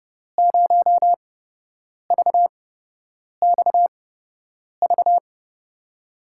Morse code, heard as a keyed tone, about 30 wpm, spaced as for 8 wpm, 700 Hz. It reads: 0VXV
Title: USA Call Signs - 8x Character Spacing - 30wpm